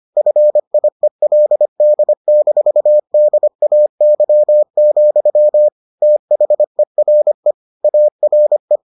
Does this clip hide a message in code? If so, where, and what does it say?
Yes: Morse code from 0.2 to 8.8 s, heard as a keyed tone, about 25 wpm, 600 Hz. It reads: FIELD-DAY, THERE ARE